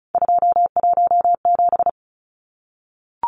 Morse code, heard as a keyed tone, about 34 wpm, 700 Hz. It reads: 217
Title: Numbers - 3-Digit 35wpm